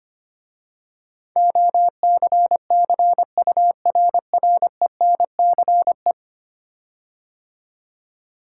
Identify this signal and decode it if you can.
Morse code, heard as a keyed tone, about 25 wpm, 700 Hz. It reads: OCCURRENCE
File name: 10 Letter Words - First 3 Letters Given - 25wpm